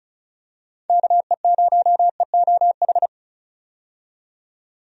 Morse code, heard as a keyed tone, about 34 wpm, 700 Hz. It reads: KE0EOH
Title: USA Call Signs - 6 Characters - 35wpm